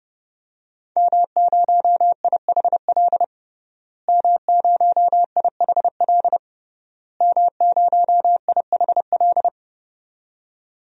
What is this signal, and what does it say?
Morse code, heard as a keyed tone, about 32 wpm, 700 Hz. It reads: M0IHL M0IHL M0IHL
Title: DXCC-100 Call Signs - 5 Characters - Repeated 3x - 30wpm